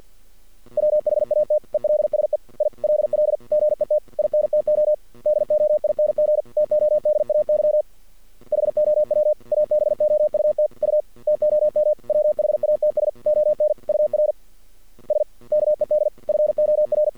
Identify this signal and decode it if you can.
Morse code, heard as a keyed tone, about 48 wpm, 600 Hz. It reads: WHAT ELSE I LL GET INTO DOING TODAY HOW ABOUT U TOM WHATS ON UR SKED FOR
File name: archived / CRSnet